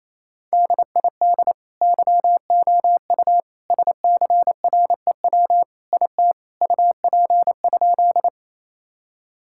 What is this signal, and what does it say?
Morse code, heard as a keyed tone, about 28 wpm, 700 Hz. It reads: DID YOU SCREW IT UP?
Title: Sentences from Top 1600 Words 28wpm